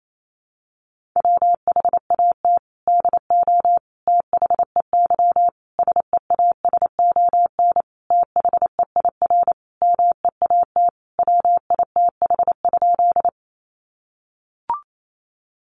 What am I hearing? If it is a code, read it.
Morse code, heard as a keyed tone, about 28 wpm, 700 Hz. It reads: WHAT DO THEY SEASON THEIR MEAT WITH?
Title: Sentences from Top 1400 Words 28wpm